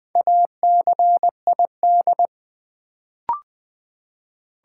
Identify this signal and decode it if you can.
Morse code, heard as a keyed tone, about 20 wpm, 700 Hz. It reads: ACID
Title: Build A Word - 2 Letters At A Time - 4-Letter Words - 20wpm